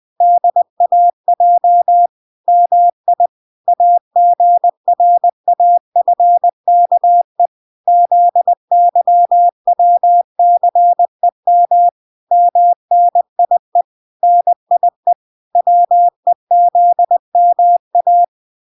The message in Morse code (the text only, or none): DAJ MI AGRAFKE ZYWCEM MNIE NIE WEZMA